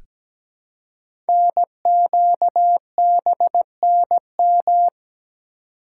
Morse code, heard as a keyed tone, about 17 wpm, 700 Hz. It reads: NQBNM
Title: Code Groups - 5 Letters - 17wpm